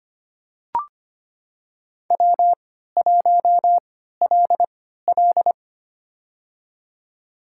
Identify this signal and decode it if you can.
Morse code, heard as a keyed tone, about 25 wpm, spaced as for 14 wpm, 700 Hz. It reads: W1LL
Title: USA Call Signs - 4 Characters - 3x Character Spacing - 25wpm